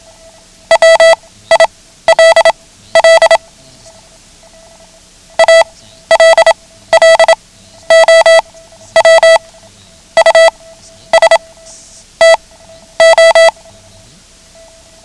Morse code, heard as a keyed tone, about 27 wpm, spaced as for 13 wpm, 660 Hz. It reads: WILL ALLOW US TO
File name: ham radio / CodeQuick / Code Quick 09